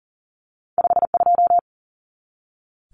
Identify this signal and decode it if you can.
Morse code, heard as a keyed tone, about 40 wpm, 700 Hz. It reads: 52
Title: Numbers - 2-Digit 40wpm